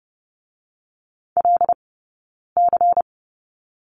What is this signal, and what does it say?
Morse code, heard as a keyed tone, about 30 wpm, spaced as for 9 wpm, 690 Hz. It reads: LC
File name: Lesson 17d - Letter u - Two Characters 30wpm